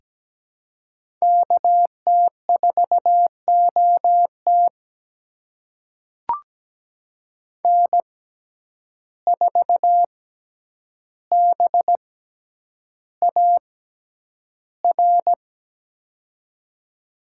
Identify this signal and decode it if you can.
Morse code, heard as a keyed tone, about 17 wpm, spaced as for 9 wpm, 700 Hz. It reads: KT4OT N4BAR
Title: USA Call Signs - 5 Characters - 6x Character Spacing - 17wpm